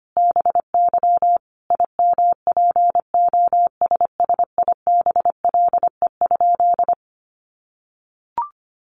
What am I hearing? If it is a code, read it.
Morse code, heard as a keyed tone, about 25 wpm, 700 Hz. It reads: BY IMPOSSIBLE?
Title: Sentences from Top 1100 Words 25wpm